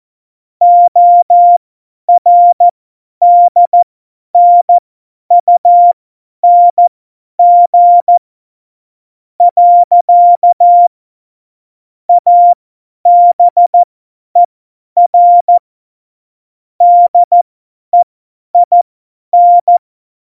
Morse code, heard as a keyed tone, about 14 wpm, spaced as for 9 wpm, 700 Hz. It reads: ORDNUNG. ABER DEIN
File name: Morsekurs / Witze / Witz 02. Januar 2018, 50 F70